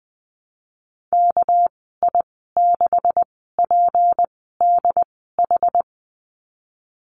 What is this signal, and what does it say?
Morse code, heard as a keyed tone, about 20 wpm, spaced as for 15 wpm, 700 Hz. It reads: KI6PDH
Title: 219 Lesson 35g - Number 0 - Call Signs 20wpm